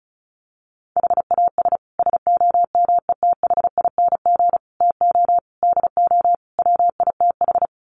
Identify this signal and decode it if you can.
Morse code, heard as a keyed tone, about 34 wpm, 700 Hz. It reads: HAS SOMETHING TO DO WITH
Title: Sets of 5 Words Encore 35wpm